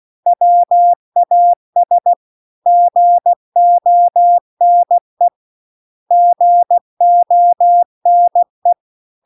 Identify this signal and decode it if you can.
Morse code, heard as a keyed tone, about 16 wpm, 700 Hz. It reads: WAS GONE GONE